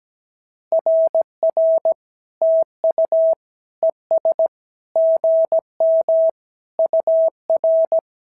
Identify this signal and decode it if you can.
Morse code, heard as a keyed tone, about 17 wpm, 650 Hz. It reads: RR TU ES GM UR
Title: SOTA QSO Practice 17wpm